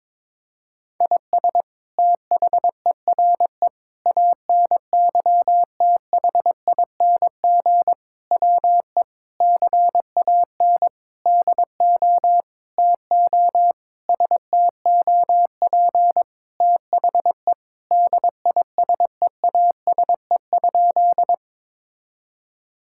Morse code, heard as a keyed tone, about 22 wpm, 700 Hz. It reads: IS THERE ANYTHING WE CAN DO TO STOP THE DISEASE?
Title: Sentences from Top 1000 Words - Review Part 6 - 22wpm